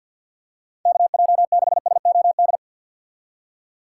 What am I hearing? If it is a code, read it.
Morse code, heard as a keyed tone, about 52 wpm, 700 Hz. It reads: KJ6SOD
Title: USA Call Signs - 6x Character Spacing - 50wpm